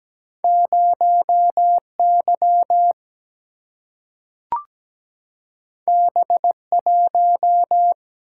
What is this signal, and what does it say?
Morse code, heard as a keyed tone, about 17 wpm, 700 Hz. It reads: 0Y B1
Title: Single-Letter Single-Number 17wpm